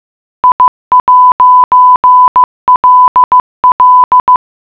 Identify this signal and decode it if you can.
Morse code, heard as a keyed tone, about 15 wpm, 1000 Hz. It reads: I'LL